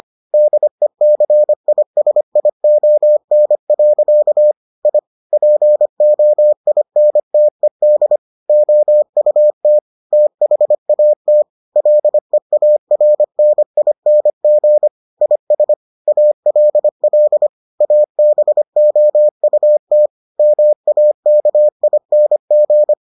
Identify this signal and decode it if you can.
Morse code, heard as a keyed tone, about 25 wpm, 600 Hz. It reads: DECISION. I POINTED OUT THAT LEARNING IS ALL ABOUT MAKING